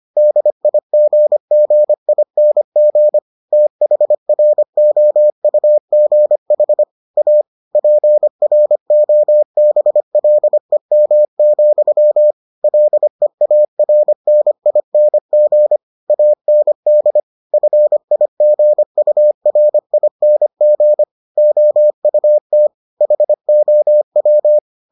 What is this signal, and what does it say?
Morse code, heard as a keyed tone, about 25 wpm, 600 Hz. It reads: DIGGING THROUGH A PROBLEM, LEARNING AND FIGURING OUT HOW